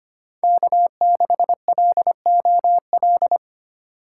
Morse code, heard as a keyed tone, about 25 wpm, 700 Hz. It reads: K6LOL
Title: USA Call Signs - 5 Characters - 4x Character Spacing - 25wpm